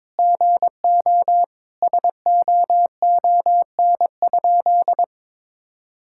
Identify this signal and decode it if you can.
Morse code, heard as a keyed tone, about 22 wpm, 700 Hz. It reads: GO SOON?
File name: practice / 22wpm / Sentences with 3-Word Phrases - 22wpm